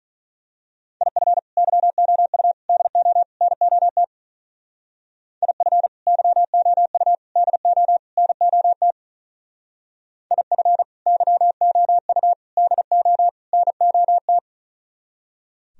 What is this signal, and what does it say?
Morse code, heard as a keyed tone, about 40 wpm, 700 Hz. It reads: IF YOU DO NOT IF YOU DO NOT IF YOU DO NOT
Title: Speed-Racing - Sets of 4 Words 35wpm